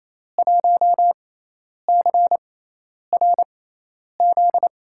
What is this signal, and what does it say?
Morse code, heard as a keyed tone, about 28 wpm, spaced as for 10 wpm, 700 Hz. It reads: 1CRZ
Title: USA Call Signs - 6x Character Spacing - 28wpm